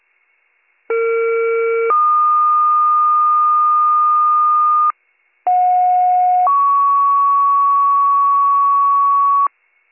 Two-tone page, heard as a one-second tone, then a three-second tone, about 460 then 1220 Hz, about 730 then 1120 Hz. It is heard from 0.9 to 9.5 s.